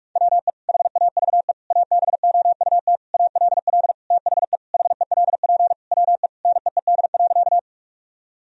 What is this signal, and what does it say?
Morse code, heard as a keyed tone, about 41 wpm, 700 Hz. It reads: WE HAVE ABOUT ALL THE HELP WE NEED.
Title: Sentences from Top 1000 Words - Review Part 1 - 45wpm